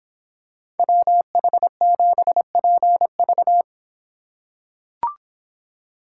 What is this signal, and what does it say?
Morse code, heard as a keyed tone, about 24 wpm, 700 Hz. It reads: WH7PV